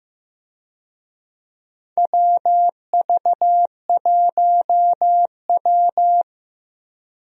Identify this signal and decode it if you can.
Morse code, heard as a keyed tone, about 15 wpm, 700 Hz. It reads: WV1W
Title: USA Call Signs - 4 Characters - Repeated 3x - 15wpm